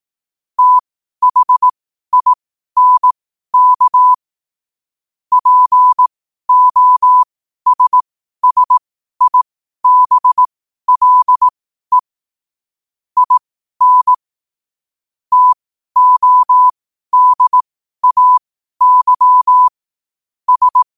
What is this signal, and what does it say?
Morse code, heard as a keyed tone, about 18 wpm, spaced as for 12 wpm, 1000 Hz. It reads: THINK POSSIBLE IN TODAY S
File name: qotdfiles / 12wpm